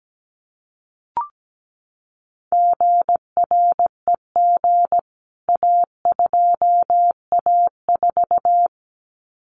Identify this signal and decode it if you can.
Morse code, heard as a keyed tone, about 17 wpm, 700 Hz. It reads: GREG A2A4
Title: CWT Exchange 17wpm